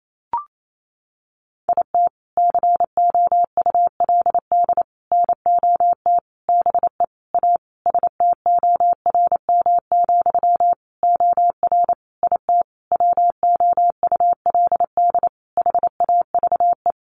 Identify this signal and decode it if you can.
Morse code, heard as a keyed tone, about 28 wpm, 700 Hz. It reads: IT COULD NOT BE A STORM, OR IT WOULD HAVE
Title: Sentences from Top 1500 Words 28wpm